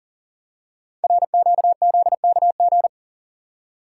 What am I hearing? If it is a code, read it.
Morse code, heard as a keyed tone, about 39 wpm, 700 Hz. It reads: RQZKG
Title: Code Groups - 5 Letter:Numbers - 40wpm